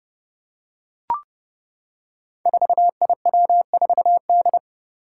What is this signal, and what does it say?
Morse code, heard as a keyed tone, about 32 wpm, 700 Hz. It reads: 4IW4D
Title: Five Challenging Letter-Number Combinations 30wpm